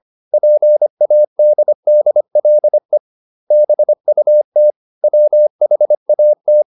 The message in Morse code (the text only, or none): PADDLE BUT WHAT